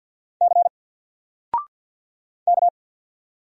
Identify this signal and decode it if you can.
Morse code, heard as a keyed tone, about 50 wpm, 700 Hz. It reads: C K
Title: Single Letter-Number 50wpm